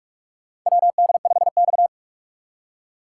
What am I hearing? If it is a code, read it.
Morse code, heard as a keyed tone, about 46 wpm, 700 Hz. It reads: WD5X
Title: USA Call Signs - 4 Characters - 6x Character Spacing - 45wpm